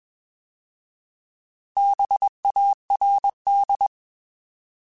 Morse code, heard as a keyed tone, about 21 wpm, 800 Hz. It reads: BARD
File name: Bs @20